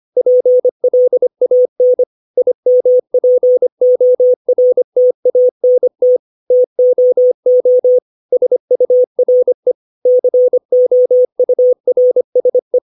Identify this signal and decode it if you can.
Morse code, heard as a keyed tone, about 25 wpm, 500 Hz. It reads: PLAN IMPORTANT TOO SURE COURSE
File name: 632 words / 632 Words 25WPM